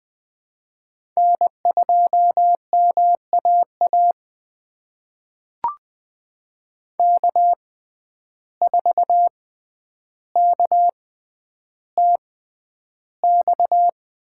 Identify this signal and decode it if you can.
Morse code, heard as a keyed tone, about 20 wpm, spaced as for 10 wpm, 700 Hz. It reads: N2MAA K4KTX
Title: USA Call Signs - 6x Character Spacing - 20wpm